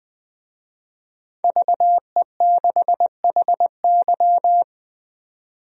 Morse code, heard as a keyed tone, about 21 wpm, 700 Hz. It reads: VE6HY
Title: DXCC-100 Call Signs - 5 Characters - 20wpm